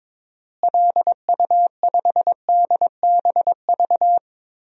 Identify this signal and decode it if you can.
Morse code, heard as a keyed tone, about 22 wpm, 700 Hz. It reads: LU5DBV
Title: DXCC-100 Call Signs - 6 Characters - Repeated 3x - 22wpm